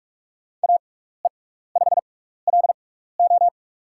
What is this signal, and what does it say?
Morse code, heard as a keyed tone, about 46 wpm, spaced as for 16 wpm, 700 Hz. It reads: AE5LO
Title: USA Call Signs - 5 Characters - 6x Character Spacing - 45wpm